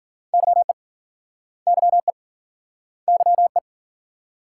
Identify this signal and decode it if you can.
Morse code, heard as a keyed tone, about 50 wpm, 700 Hz. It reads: YE YE YE
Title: Speed-Racing Multi-letter Phonemes - 40wpm